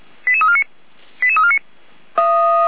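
Five-tone selective call (ZVEI), heard as quick runs of five tones, tones of about 70 ms. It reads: 88349 88349
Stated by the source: ZVEI five-tone selective call received on SDR.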